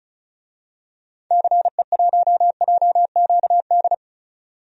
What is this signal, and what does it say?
Morse code, heard as a keyed tone, about 36 wpm, 700 Hz. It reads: CE1JQD
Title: DXCC-100 Call Signs - 6 Characters - 35wpm